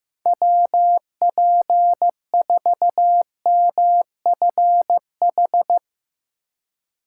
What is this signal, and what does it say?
Morse code, heard as a keyed tone, about 15 wpm, 700 Hz. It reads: WP4MFH
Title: USA Call Signs - 3x Character Spacing - 15wpm